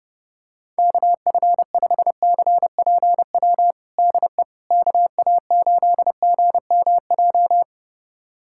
Morse code, heard as a keyed tone, about 30 wpm, 700 Hz. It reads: KF5CPW DE KA8GMJ
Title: USA Callsign DE Callsign - 6 Characters - 30wpm